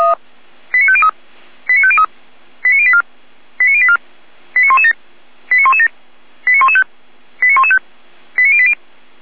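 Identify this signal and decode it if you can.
Five-tone selective call (ZVEI), heard as quick runs of five tones, tones of about 70 ms. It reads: 89503 89503 89084 89084 89117 89117 89115 89115 89080
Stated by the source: ZVEI five-tone selective call received on an SDR.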